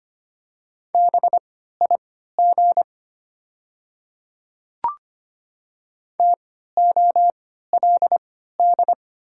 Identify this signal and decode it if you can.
Morse code, heard as a keyed tone, about 25 wpm, spaced as for 14 wpm, 700 Hz. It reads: BIG TOLD
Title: 3x Character Spacing - Top 500 Words - 25wpm